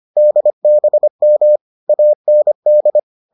Morse code, heard as a keyed tone, about 25 wpm, 600 Hz. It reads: DBM AND